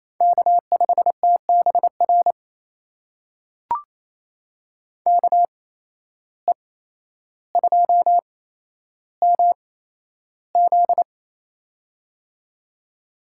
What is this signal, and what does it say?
Morse code, heard as a keyed tone, about 28 wpm, spaced as for 12 wpm, 700 Hz. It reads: K5TBR KE2MZ
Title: USA Call Signs - 5 Characters - 8x Character Spacing - 28wpm